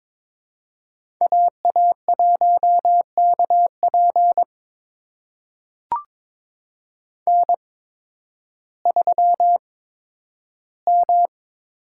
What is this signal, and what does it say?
Morse code, heard as a keyed tone, about 22 wpm, spaced as for 12 wpm, 700 Hz. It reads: AA1KP N3M
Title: USA Call Signs - 5 Characters - 8x Character Spacing - 22wpm